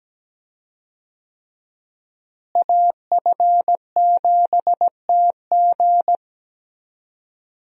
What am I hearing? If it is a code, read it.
Morse code, heard as a keyed tone, about 17 wpm, 700 Hz. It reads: AF7TG
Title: USA Call Signs - 5 Characters - Repeated 3x - 17wpm